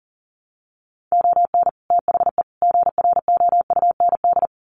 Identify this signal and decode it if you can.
Morse code, heard as a keyed tone, about 40 wpm, 700 Hz. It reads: ON THE GROUND
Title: Sets of 3 Words 40wpm